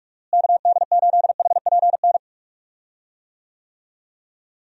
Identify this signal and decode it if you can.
Morse code, heard as a keyed tone, about 46 wpm, 700 Hz. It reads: KD8HPN
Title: USA Call Signs - 6 Characters - Repeated 3x - 45wpm